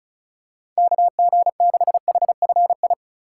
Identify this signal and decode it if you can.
Morse code, heard as a keyed tone, about 34 wpm, 700 Hz. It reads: KG6HFI
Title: USA Call Signs - 6 Characters - 4x Character Spacing - 35wpm